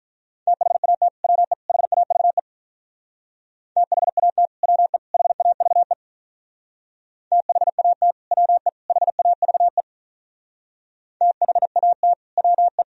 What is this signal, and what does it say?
Morse code, heard as a keyed tone, about 46 wpm, 700 Hz. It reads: THAT WE HAVE THAT WE HAVE THAT WE HAVE THAT WE